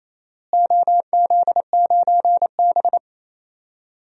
Morse code, heard as a keyed tone, about 28 wpm, 700 Hz. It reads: OZ9B
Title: DXCC-100 Call Signs - 4 Characters - Repeated 2x - 28wpm